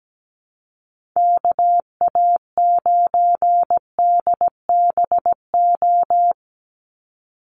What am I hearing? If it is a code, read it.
Morse code, heard as a keyed tone, about 17 wpm, 700 Hz. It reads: KA9DBO
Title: USA Call Signs 17wpm